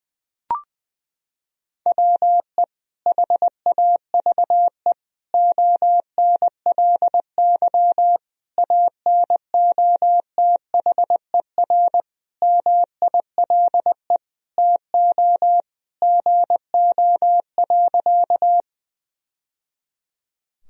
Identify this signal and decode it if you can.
Morse code, heard as a keyed tone, about 20 wpm, 700 Hz. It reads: WE HAVE ONLY ANOTHER MILE TO GO.
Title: Sentences from Top 1000 Words - Review Part 1 - 20wpm